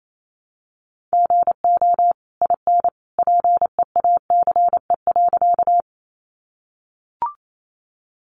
Morse code, heard as a keyed tone, about 28 wpm, 700 Hz. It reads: GO IN PEACE.
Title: Sentences from Top 1500 Words 28wpm